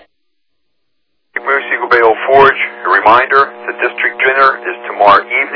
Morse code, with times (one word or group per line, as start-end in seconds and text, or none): none